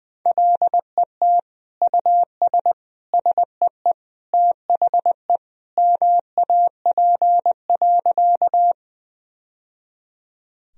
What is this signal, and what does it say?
Morse code, heard as a keyed tone, about 20 wpm, 700 Hz. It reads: LET US SEE THE MAP.
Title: Sentences from Top 1000 Words - Review Part 2 - 20wpm